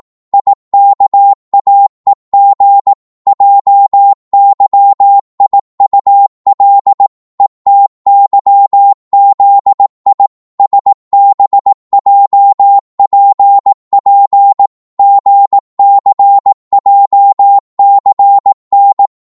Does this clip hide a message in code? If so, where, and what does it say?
Yes: Morse code from 0.3 to 19.1 s, heard as a keyed tone, about 18 wpm, 800 Hz. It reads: IKAEG JYIUL ETYZI SBJPP GCJCN